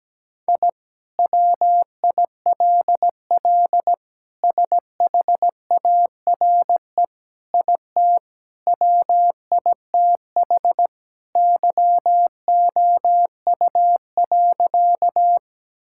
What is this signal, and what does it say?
Morse code, heard as a keyed tone, about 17 wpm, 700 Hz. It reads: I WILL SHARE IT WITH YOU.
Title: Sentences from Top 1000 Words - Review Part 6 - 17wpm